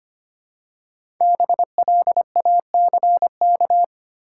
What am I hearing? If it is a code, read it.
Morse code, heard as a keyed tone, about 25 wpm, 700 Hz. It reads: BLACK
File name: 5 Letter Words - First 3 Letters Given - 25wpm